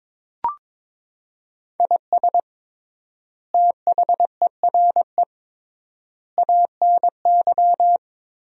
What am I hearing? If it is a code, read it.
Morse code, heard as a keyed tone, about 22 wpm, 700 Hz. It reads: IS THERE ANY
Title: 3x Word-Spacing - Sentences from Top 300 Words - 22wpm